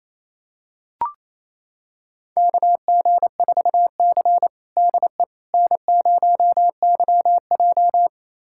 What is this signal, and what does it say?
Morse code, heard as a keyed tone, about 28 wpm, 700 Hz. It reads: KG4C DE N0YJ
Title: USA Callsign DE Callsign - 4 Characters - 28wpm